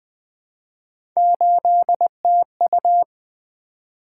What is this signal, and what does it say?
Morse code, heard as a keyed tone, about 20 wpm, 700 Hz. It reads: 8TU
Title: Three Letter-Number Combinations - 20wpm